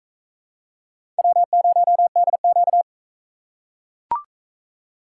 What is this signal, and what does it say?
Morse code, heard as a keyed tone, about 41 wpm, 700 Hz. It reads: W0DQ